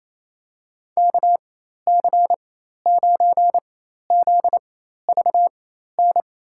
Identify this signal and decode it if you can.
Morse code, heard as a keyed tone, about 28 wpm, spaced as for 13 wpm, 700 Hz. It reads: KC9ZVN
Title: USA Call Signs - 6 Characters - 4x Character Spacing - 28wpm